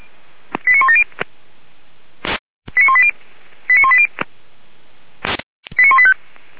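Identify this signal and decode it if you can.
Five-tone selective call (ZVEI), heard as quick runs of five tones, tones of about 70 ms. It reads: 89170 89180 89180 89185
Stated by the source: ZVEI five-tone call received on SDR by feuerwehr-orlamuende fire service.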